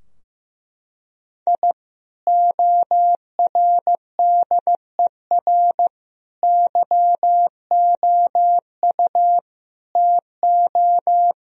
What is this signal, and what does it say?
Morse code, heard as a keyed tone, about 15 wpm, 700 Hz. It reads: I ORDER YOU TO
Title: Sentences from Top 1000 Words - Review Part 2 - 15wpm